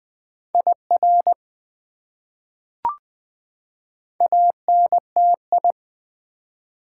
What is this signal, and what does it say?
Morse code, heard as a keyed tone, about 21 wpm, 700 Hz. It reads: IR ANTI
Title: Prefixes and Suffixes 20wpm